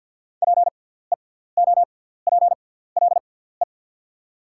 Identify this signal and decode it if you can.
Morse code, heard as a keyed tone, about 54 wpm, spaced as for 17 wpm, 690 Hz. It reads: PEOPLE
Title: 6x Character Spacing - Top 300 Words - 50wpm